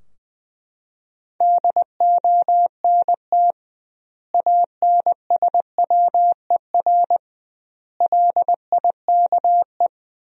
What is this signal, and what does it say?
Morse code, heard as a keyed tone, about 19 wpm, 700 Hz. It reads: DONT ANSWER LIKE
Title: 2x Word-Spacing - Sentences from Top 400 Words - 20wpm